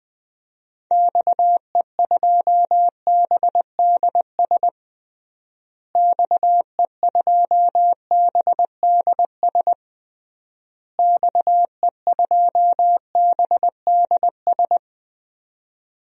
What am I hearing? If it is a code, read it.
Morse code, heard as a keyed tone, about 20 wpm, 700 Hz. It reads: XE2BDS XE2BDS XE2BDS
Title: DXCC-100 Call Signs - 6 Characters - Repeated 3x - 20wpm